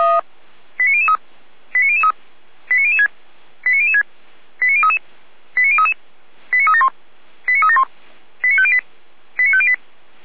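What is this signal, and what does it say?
Five-tone selective call (ZVEI), heard as quick runs of five tones, tones of about 70 ms. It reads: 89003 89003 89006 89006 89033 89033 89371 89371 89508 89508
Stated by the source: ZVEI five-tone selective call received with an SDR, fire brigade feuerwehr-orlamuende.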